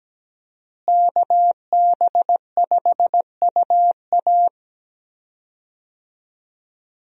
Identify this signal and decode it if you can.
Morse code, heard as a keyed tone, about 17 wpm, 700 Hz. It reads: KB5UA
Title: USA Call Signs - 5 Characters - Repeated 3x - 17wpm